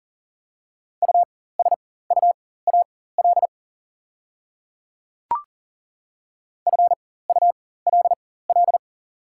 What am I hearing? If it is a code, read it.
Morse code, heard as a keyed tone, about 38 wpm, spaced as for 18 wpm, 690 Hz. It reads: USUAL FULL
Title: 4x Character Spacing - Top 500 Words - 40wpm